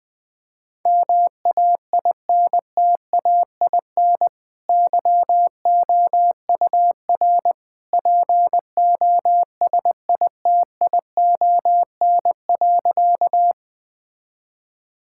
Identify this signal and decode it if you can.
Morse code, heard as a keyed tone, about 21 wpm, 700 Hz. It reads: MAINTAIN YOUR POSITION.
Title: Sentences from Top 1000 Words - Review Part 5 - 20wpm